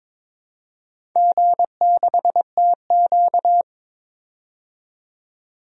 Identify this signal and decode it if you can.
Morse code, heard as a keyed tone, about 22 wpm, 700 Hz. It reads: G6TQ
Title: DXCC-100 Call Signs - 4 Characters - 22wpm